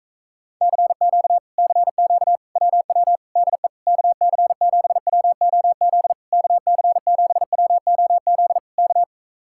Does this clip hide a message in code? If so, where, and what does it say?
Yes: Morse code from 0.6 to 9.0 s, heard as a keyed tone, about 45 wpm, 700 Hz. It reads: CQ CQ WW DE KC7WOZ KC7WOZ K